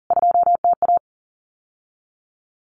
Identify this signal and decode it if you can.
Morse code, heard as a keyed tone, about 40 wpm, 700 Hz. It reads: 2TA
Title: CWT Exchange 40wpm